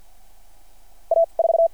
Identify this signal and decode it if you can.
Morse code, heard as a keyed tone, about 53 wpm, 660 Hz. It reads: A 4
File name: archived / CRSnet